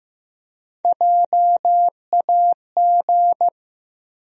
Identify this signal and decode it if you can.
Morse code, heard as a keyed tone, about 15 wpm, 700 Hz. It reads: JAG
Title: USA Call Sign Postfixes - 3 Letters 15wpm